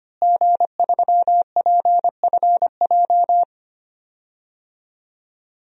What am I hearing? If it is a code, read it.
Morse code, heard as a keyed tone, about 25 wpm, 700 Hz. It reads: G3PFJ
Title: DXCC-100 Call Signs - 5 Characters - Repeated 2x - 25wpm